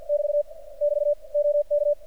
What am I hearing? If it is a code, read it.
Morse code, heard as a keyed tone, about 45 wpm, 590 Hz. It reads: <BT> <BT> OK
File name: archived / CRSnet